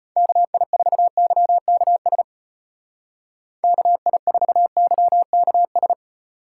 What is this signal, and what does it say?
Morse code, heard as a keyed tone, about 33 wpm, 700 Hz. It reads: KI4YKS KI4YKS